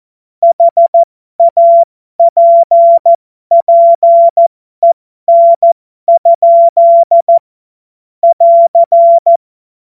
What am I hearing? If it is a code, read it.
Morse code, heard as a keyed tone, about 14 wpm, spaced as for 12 wpm, 680 Hz. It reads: HAPPEN? <AR>
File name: Morsekurs / Witze